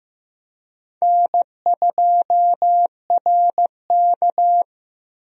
Morse code, heard as a keyed tone, about 15 wpm, 700 Hz. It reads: N2RK
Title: USA Call Signs - 4 Characters - 15wpm